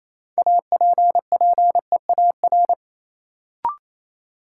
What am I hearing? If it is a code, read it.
Morse code, heard as a keyed tone, about 28 wpm, 700 Hz. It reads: APPEAR
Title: Sentences from Top 1000 Words - Review Part 2 - New Word Drill - 28wpm